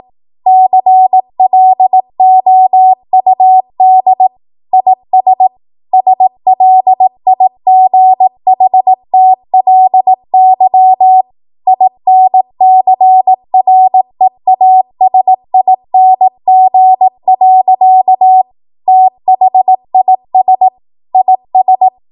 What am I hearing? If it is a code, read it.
Morse code, heard as a keyed tone, about 18 wpm, 750 Hz. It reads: CLOUD IS SLIGHTLY INCREASING. THIS IS
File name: Morse / 18WPM